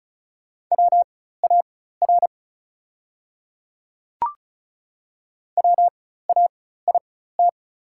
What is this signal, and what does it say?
Morse code, heard as a keyed tone, about 36 wpm, spaced as for 16 wpm, 690 Hz. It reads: WAR WAIT
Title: 4x Character Spacing - Top 500 Words - 35wpm